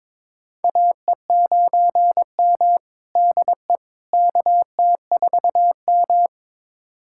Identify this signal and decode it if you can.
Morse code, heard as a keyed tone, about 22 wpm, 700 Hz. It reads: AE9M DE KT4M
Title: USA Callsign DE Callsign - 4 Characters - 22wpm